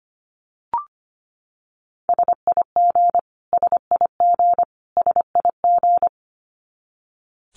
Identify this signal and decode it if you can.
Morse code, heard as a keyed tone, about 25 wpm, 700 Hz. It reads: SIG SIG SIG
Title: Abbreviations Repeated 3x - 25wpm